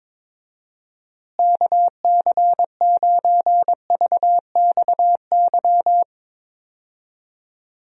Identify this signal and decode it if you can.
Morse code, heard as a keyed tone, about 21 wpm, 700 Hz. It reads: KC9VXY
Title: USA Call Signs - 6 Characters - Repeated 2x - 22wpm